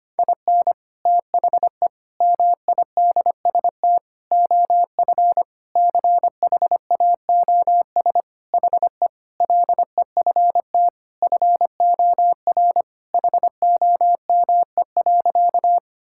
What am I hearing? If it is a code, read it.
Morse code, heard as a keyed tone, about 25 wpm, 700 Hz. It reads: IN THE MIDST OF CHAOS HE LEFT FOR HOME.
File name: practice / 25wpm / Sentences with 4-Word Phrases - 25wpm